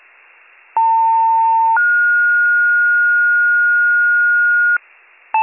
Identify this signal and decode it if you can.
Two-tone page, heard as a one-second tone, then a three-second tone, about 900 then 1430 Hz, with noise. It is heard from 0.8 to 4.8 s.